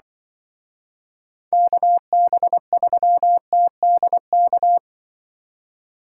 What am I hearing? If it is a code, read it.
Morse code, heard as a keyed tone, about 23 wpm, 700 Hz. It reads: KB3TDK
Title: Speed Racing - USA 2x3 Call Signs 20wpm